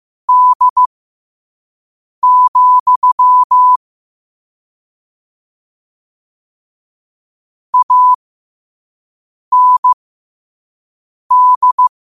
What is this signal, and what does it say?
Morse code, heard as a keyed tone, about 15 wpm, spaced as for 5 wpm, 1000 Hz. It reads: D, AND
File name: 5wpm